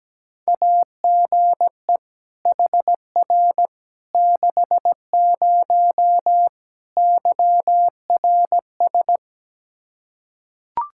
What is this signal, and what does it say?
Morse code, heard as a keyed tone, about 17 wpm, 700 Hz. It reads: AGE HR 60 YRS
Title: Practice QSO - Exchange 3 - More Info - 17wpm